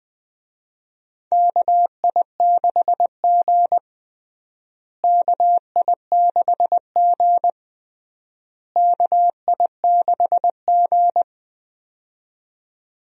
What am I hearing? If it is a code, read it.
Morse code, heard as a keyed tone, about 20 wpm, 690 Hz. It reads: KI6G KI6G KI6G
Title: USA Call Signs - 4 Characters - Repeated 3x - 20wpm